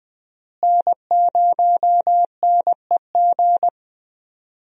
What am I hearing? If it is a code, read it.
Morse code, heard as a keyed tone, about 21 wpm, 700 Hz. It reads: N0NEG
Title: USA Call Signs - 5 Characters - Repeated 3x - 20wpm